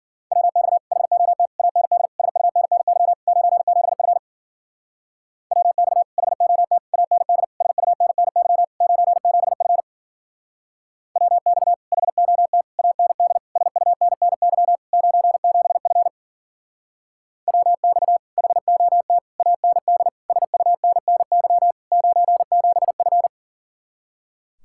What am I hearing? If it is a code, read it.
Morse code, heard as a keyed tone, about 50 wpm, 700 Hz. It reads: WX HOT AND SUNNY 97F WX HOT AND SUNNY 97F WX HOT AND SUNNY 97F WX HOT AND SUNNY 97F
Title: Speed-Racing Common QSO Phrases Encore - 40wpm